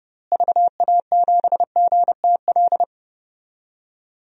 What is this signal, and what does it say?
Morse code, heard as a keyed tone, about 32 wpm, 700 Hz. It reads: VA7GTL
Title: DXCC-100 Call Signs - 6 Characters - Repeated 3x - 30wpm